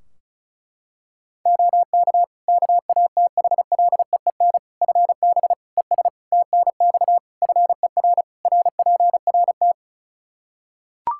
Morse code, heard as a keyed tone, about 36 wpm, 700 Hz. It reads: OK KATHLEEN FB ES TNX FER RPRT
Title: Practice QSO - Exchange 2 - Info - 35wpm